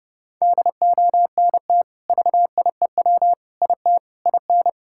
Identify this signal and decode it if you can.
Morse code, heard as a keyed tone, about 30 wpm, 700 Hz. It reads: DONT VIEW IT IN
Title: Sentences from Top 1000 Words - Review Part 6 - 30wpm